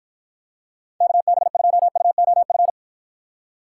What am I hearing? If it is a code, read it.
Morse code, heard as a keyed tone, about 52 wpm, 700 Hz. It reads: KB2UOF